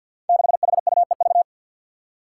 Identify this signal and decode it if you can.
Morse code, heard as a keyed tone, about 52 wpm, 700 Hz. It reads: 6HUEV
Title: Five Challenging Letter-Number Combinations 50wpm